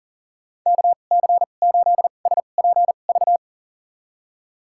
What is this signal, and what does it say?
Morse code, heard as a keyed tone, about 38 wpm, spaced as for 28 wpm, 700 Hz. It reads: KC8SPV
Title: USA Call Signs - 2x Character Spacing - 40wpm